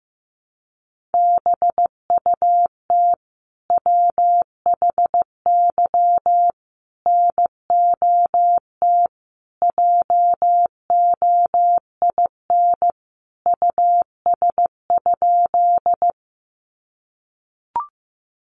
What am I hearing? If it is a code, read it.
Morse code, heard as a keyed tone, about 14 wpm, 700 Hz. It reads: BUT WHY NOT JOIN US?
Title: Sentences from Top 1400 Words 15wpm